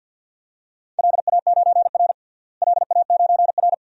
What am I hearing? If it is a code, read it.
Morse code, heard as a keyed tone, about 53 wpm, 700 Hz. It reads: LA9R LA9R
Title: DXCC-100 Call Signs - 4 Characters - Repeated 2x - 50wpm